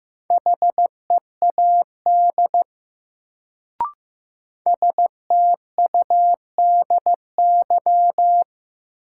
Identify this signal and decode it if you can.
Morse code, heard as a keyed tone, about 15 wpm, 700 Hz. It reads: HEAD STUDY
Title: Sentences from Top 300 Words - New Word Drill - 15wpm